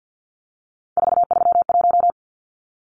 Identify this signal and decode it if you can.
Morse code, heard as a keyed tone, about 50 wpm, 700 Hz. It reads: 431
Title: Numbers - 3-Digit 50wpm